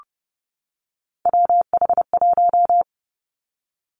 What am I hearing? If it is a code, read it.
Morse code, heard as a keyed tone, about 30 wpm, 700 Hz. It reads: WH1
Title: Three Challenging Letter-Number Combinations 30wpm